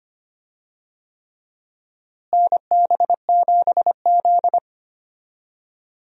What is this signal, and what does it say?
Morse code, heard as a keyed tone, about 25 wpm, 700 Hz. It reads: NB7Z
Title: USA Call Signs - 4 Characters - Repeated 3x - 25wpm